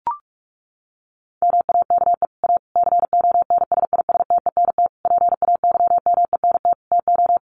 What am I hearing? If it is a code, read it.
Morse code, heard as a keyed tone, about 44 wpm, 700 Hz. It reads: MAKE A CONSISTENT PAYMENT TO
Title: Sentences from Top 1500 Words 45wpm